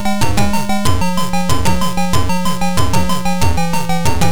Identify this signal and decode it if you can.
There is no code.